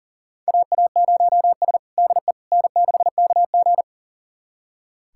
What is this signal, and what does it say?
Morse code, heard as a keyed tone, about 40 wpm, 700 Hz. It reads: AA0S DE N6KG
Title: USA Callsign DE Callsign - 4 Characters - 40wpm